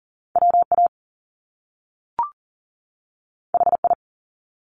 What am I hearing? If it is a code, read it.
Morse code, heard as a keyed tone, about 40 wpm, 700 Hz. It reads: WA HI
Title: US States 40wpm